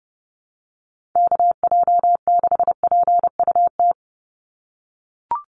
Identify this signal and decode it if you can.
Morse code, heard as a keyed tone, about 29 wpm, 700 Hz. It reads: KJ6PUT
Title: USA Call Signs 30wpm